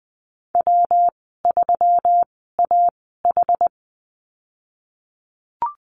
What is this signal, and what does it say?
Morse code, heard as a keyed tone, about 20 wpm, spaced as for 15 wpm, 700 Hz. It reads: W3AH
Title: 122 Lesson 20g - Number 3 - Call Signs 20wpm